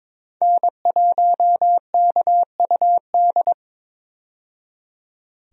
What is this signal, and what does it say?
Morse code, heard as a keyed tone, about 21 wpm, 700 Hz. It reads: N1KUD
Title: USA Call Signs - 5 Characters - 22wpm